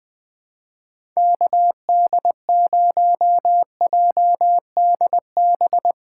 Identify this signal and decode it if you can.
Morse code, heard as a keyed tone, about 20 wpm, 700 Hz. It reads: KD0JDB
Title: USA Callsign DE Callsign - 6 Characters - 20wpm